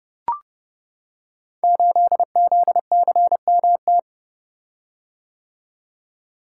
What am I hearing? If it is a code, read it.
Morse code, heard as a keyed tone, about 29 wpm, 700 Hz. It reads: 8ZCMT
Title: Code Groups - 5 Letter:Numbers - 30wpm